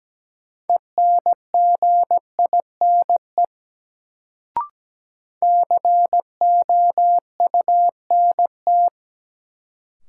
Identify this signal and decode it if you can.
Morse code, heard as a keyed tone, about 17 wpm, 700 Hz. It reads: ENGINE COUNT
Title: Sentences from Top 1500 Words - New Word Drill - 17wpm